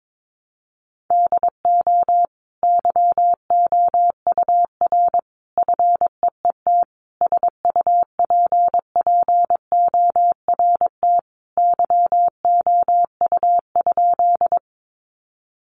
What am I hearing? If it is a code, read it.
Morse code, heard as a keyed tone, about 22 wpm, 700 Hz. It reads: DO YOUR FEET SUPPORT YOU?
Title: Sentences from Top 800 Words 22wpm